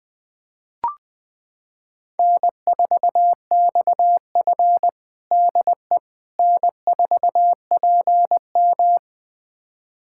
Morse code, heard as a keyed tone, about 20 wpm, 700 Hz. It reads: N4XF DE N4PM
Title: USA Callsign DE Callsign - 4 Characters - 20wpm